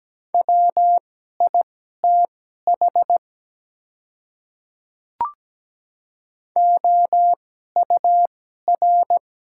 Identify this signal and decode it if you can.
Morse code, heard as a keyed tone, about 17 wpm, spaced as for 12 wpm, 700 Hz. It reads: WITH OUR
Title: 2x Character Spacing - Top 200 Words - 17wpm